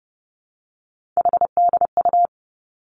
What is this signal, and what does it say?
Morse code, heard as a keyed tone, about 30 wpm, 700 Hz. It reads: HDU
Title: Three Challenging Letter-Number Combinations 30wpm